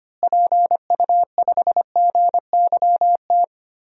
Morse code, heard as a keyed tone, about 25 wpm, 700 Hz. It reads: PU5GYT
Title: DXCC-100 Call Signs - 6 Characters - Repeated 2x - 25wpm